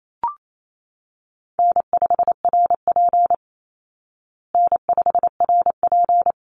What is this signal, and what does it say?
Morse code, heard as a keyed tone, about 28 wpm, 700 Hz. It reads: N5RP N5RP
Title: Call Signs Repeated 3x - 28wpm